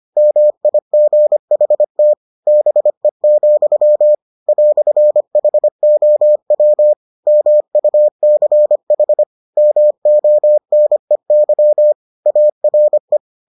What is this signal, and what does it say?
Morse code, heard as a keyed tone, about 25 wpm, 600 Hz. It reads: MIGHT BE, "HOW MUCH MONEY ARE